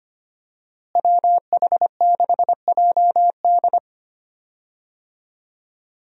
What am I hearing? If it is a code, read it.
Morse code, heard as a keyed tone, about 25 wpm, 700 Hz. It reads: WH6JD
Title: USA Call Signs - 5 Characters - Repeated 3x - 25wpm